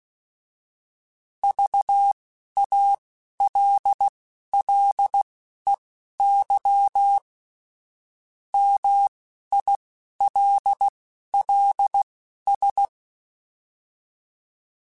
Morse code, heard as a keyed tone, about 16 wpm, spaced as for 11 wpm, 800 Hz. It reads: VALLEY MILLS